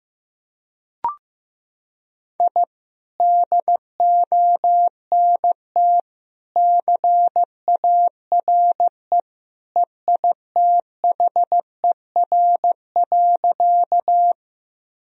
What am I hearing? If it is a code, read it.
Morse code, heard as a keyed tone, about 15 wpm, 700 Hz. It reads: I DONT CARE EITHER.
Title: Sentences from Top 1000 Words - Review Part 4 - 15wpm